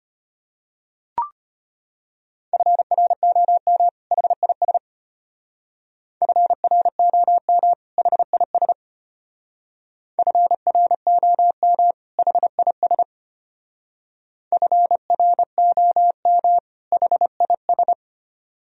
Morse code, heard as a keyed tone, about 33 wpm, 700 Hz. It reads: FROM HIS FROM HIS FROM HIS FROM HIS